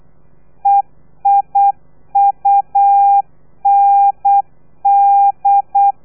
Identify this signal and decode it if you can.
Morse code, heard as a keyed tone, about 8 wpm, 800 Hz. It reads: EIUND
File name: telegrafia / i2tfj / Lez3 35